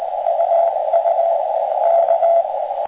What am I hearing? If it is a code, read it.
Morse code, heard as a keyed tone, about 20 wpm, 670 Hz, with radio noise. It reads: UU K